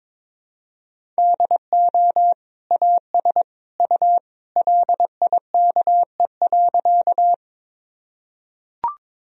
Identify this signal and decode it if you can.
Morse code, heard as a keyed tone, about 22 wpm, 700 Hz. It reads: DO AS U LIKE.
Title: Sentences from Top 100 Words and Abbreviations 22wpm